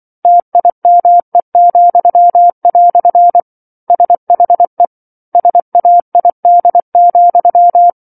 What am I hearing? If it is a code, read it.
Morse code, heard as a keyed tone, about 24 wpm, 700 Hz. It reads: TIME," SHE SAID,